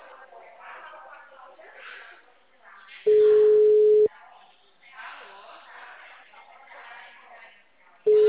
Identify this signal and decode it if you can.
Ringback tone (Europe, 425 Hz), with speech.
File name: out- -2044- - - .11385